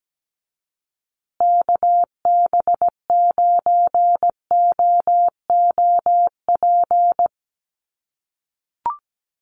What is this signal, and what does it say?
Morse code, heard as a keyed tone, about 17 wpm, 700 Hz. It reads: KB9OOP
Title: USA Call Signs Repeated 3x 17wpm